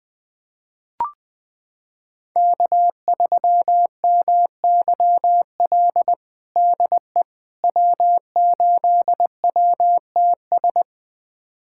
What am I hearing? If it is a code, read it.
Morse code, heard as a keyed tone, about 20 wpm, 700 Hz. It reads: K3MYL DE W8WTS
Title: USA Callsign DE Callsign - 5 Characters - 20wpm